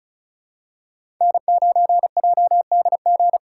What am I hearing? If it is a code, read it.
Morse code, heard as a keyed tone, about 34 wpm, 700 Hz. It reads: N9JDG
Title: USA Call Signs - 5 Characters - 3x Character Spacing - 35wpm